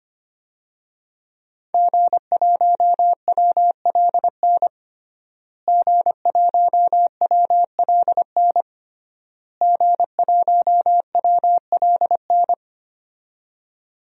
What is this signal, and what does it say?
Morse code, heard as a keyed tone, about 25 wpm, 700 Hz. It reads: G1WLN G1WLN G1WLN
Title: DXCC-100 Call Signs - 5 Characters - Repeated 3x - 25wpm